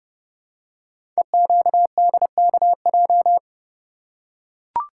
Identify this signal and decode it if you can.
Morse code, heard as a keyed tone, about 30 wpm, 700 Hz. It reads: EQDKJ
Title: Code Groups - 5 Letters - 30wpm